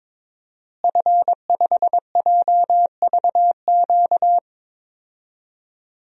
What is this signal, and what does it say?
Morse code, heard as a keyed tone, about 22 wpm, 700 Hz. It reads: F5JVQ
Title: DXCC-100 Call Signs - 5 Characters - Repeated 2x - 22wpm